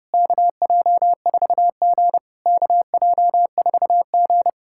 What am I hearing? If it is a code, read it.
Morse code, heard as a keyed tone, about 30 wpm, 690 Hz. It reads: KJ4G KJ4G
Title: SOTA QSO Practice 30wpm